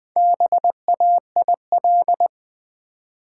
Morse code, heard as a keyed tone, about 20 wpm, 700 Hz. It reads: BAIL
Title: Build A Word - 1 Letter At A Time - 4-Letter Words - 20wpm